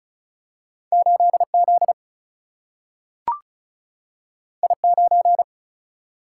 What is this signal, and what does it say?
Morse code, heard as a keyed tone, about 36 wpm, 700 Hz. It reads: 8 Z I 9
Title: Single-Letter Single-Number 35wpm